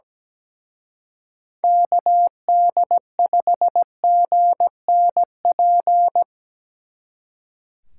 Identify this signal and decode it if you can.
Morse code, heard as a keyed tone, about 20 wpm, 700 Hz. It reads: KD5GNP
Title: Speed Racing - USA 2x3 Call Signs 17wpm